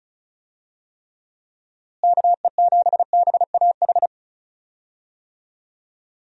Morse code, heard as a keyed tone, about 34 wpm, 700 Hz. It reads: KE7BAH
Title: USA Call Signs - 6 Characters - Repeated 2x - 35wpm